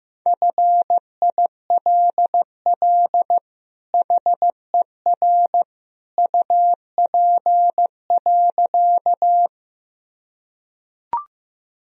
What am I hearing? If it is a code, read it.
Morse code, heard as a keyed tone, about 15 wpm, 700 Hz. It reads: FILL HER UP.
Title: Sentences from Top 1000 Words - Review Part 3 - 15wpm